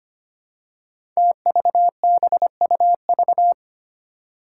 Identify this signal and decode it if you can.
Morse code, heard as a keyed tone, about 25 wpm, 700 Hz. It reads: TVBUV
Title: Five Challenging Letter-Number Combinations 25wpm